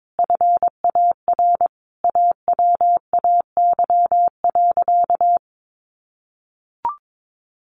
Morse code, heard as a keyed tone, about 22 wpm, 700 Hz. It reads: FAR AWAY.
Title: Sentences from Top 600 Words 22wpm